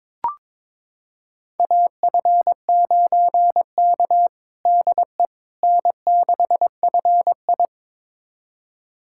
Morse code, heard as a keyed tone, about 22 wpm, 700 Hz. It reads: AF9K DE N6FI
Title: USA Callsign DE Callsign - 4 Characters - 22wpm